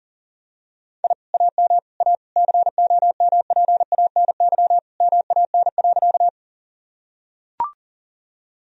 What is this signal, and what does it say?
Morse code, heard as a keyed tone, about 37 wpm, 700 Hz. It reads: I AM A COMPANY MAN.
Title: Sentences from Top 1000 Words - Review Part 3 - 40wpm